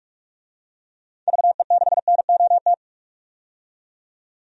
Morse code, heard as a keyed tone, about 46 wpm, 700 Hz. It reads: VE6NOT
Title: DXCC-100 Call Signs - 6 Characters - 45wpm